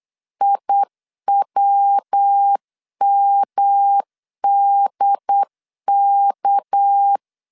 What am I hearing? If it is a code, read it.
Morse code, heard as a keyed tone, about 8 wpm, 800 Hz. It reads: IWMDK
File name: LEZIONI / 6 PKX0 / 6 PKX0 8wpm 3341